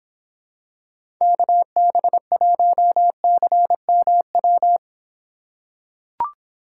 Morse code, heard as a keyed tone, about 26 wpm, 700 Hz. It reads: KB1CMW